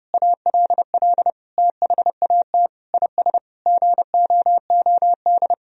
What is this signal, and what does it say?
Morse code, heard as a keyed tone, about 30 wpm, 700 Hz. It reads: ALL THAT IS GOOD
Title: Sentences from Top 1000 Words - Review Part 3 - 30wpm